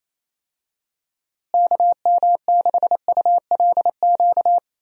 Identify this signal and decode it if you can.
Morse code, heard as a keyed tone, about 27 wpm, 700 Hz. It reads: KM6ULQ
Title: USA Call Signs - 6 Characters - Repeated 3x - 28wpm